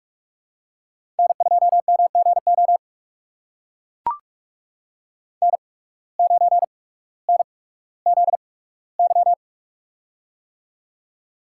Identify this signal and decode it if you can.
Morse code, heard as a keyed tone, about 45 wpm, spaced as for 19 wpm, 700 Hz. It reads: N2MGO N9NZY
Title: USA Call Signs - 8x Character Spacing - 45wpm